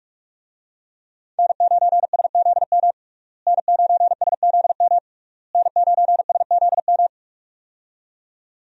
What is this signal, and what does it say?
Morse code, heard as a keyed tone, about 46 wpm, 700 Hz. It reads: N9SZM N9SZM N9SZM
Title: USA Call Signs - 5 Characters - Repeated 3x - 45wpm